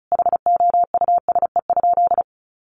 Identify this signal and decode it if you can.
Morse code, heard as a keyed tone, about 34 wpm, 700 Hz. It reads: HOUSE?
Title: Sentences from Top 1400 Words 35wpm